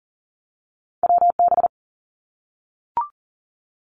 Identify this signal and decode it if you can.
Morse code, heard as a keyed tone, about 38 wpm, 700 Hz. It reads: WB
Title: Two Challenging Letter-Number Combinations 40wpm